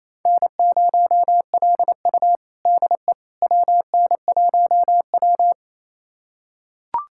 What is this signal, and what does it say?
Morse code, heard as a keyed tone, about 28 wpm, 700 Hz. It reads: N0LU DE WN1W
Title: USA Callsign DE Callsign - 4 Characters - 28wpm